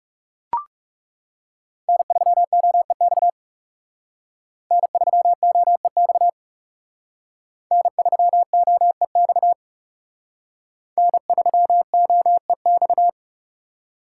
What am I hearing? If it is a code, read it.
Morse code, heard as a keyed tone, about 38 wpm, 690 Hz. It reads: N3OEX N3OEX N3OEX N3OEX